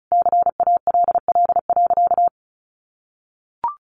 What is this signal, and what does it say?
Morse code, heard as a keyed tone, about 34 wpm, 700 Hz. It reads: CALL.
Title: Sentences from Top 900 Words 35wpm